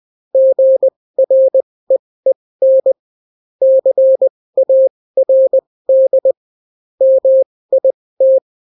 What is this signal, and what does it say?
Morse code, heard as a keyed tone, about 20 wpm, spaced as for 16 wpm, 530 Hz. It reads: GREEN CARD MIT